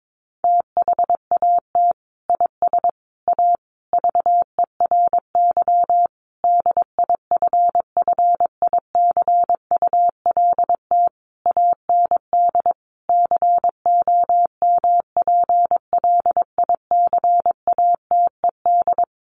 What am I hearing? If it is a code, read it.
Morse code, heard as a keyed tone, about 22 wpm, 700 Hz. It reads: THAT IS A VERY DIFFICULT AND COMPLICATED
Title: Sentences from Top 1200 Words 22wpm